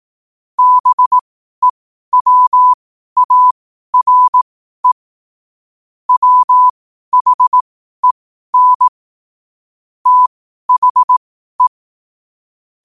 Morse code, heard as a keyed tone, about 18 wpm, spaced as for 12 wpm, 1000 Hz. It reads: BEWARE WHEN THE
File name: 12wpm